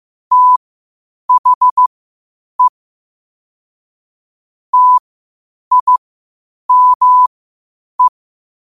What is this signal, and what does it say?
Morse code, heard as a keyed tone, about 15 wpm, spaced as for 8 wpm, 1000 Hz. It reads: THE TIME
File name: qotdfiles / 8wpm